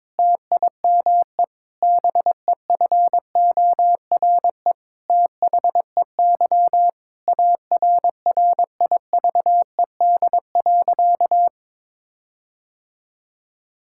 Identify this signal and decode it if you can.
Morse code, heard as a keyed tone, about 22 wpm, 700 Hz. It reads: TIME BEFORE THEY ARRIVED.
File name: Sentences with 7-Word Phrases - 22wpm